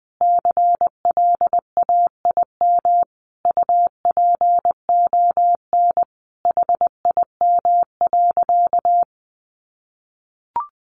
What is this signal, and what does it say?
Morse code, heard as a keyed tone, about 20 wpm, 700 Hz. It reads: CLAIM UPON HIM.
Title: Sentences from Top 1500 Words 20wpm